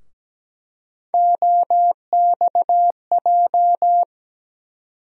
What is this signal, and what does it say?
Morse code, heard as a keyed tone, about 17 wpm, 700 Hz. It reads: OXJ
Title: USA Call Sign Postfixes - 3 Letters 17wpm